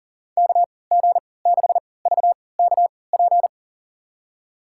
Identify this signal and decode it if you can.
Morse code, heard as a keyed tone, about 40 wpm, spaced as for 23 wpm, 700 Hz. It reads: KG6VKP
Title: USA Call Signs - 6 Characters - 3x Character Spacing - 40wpm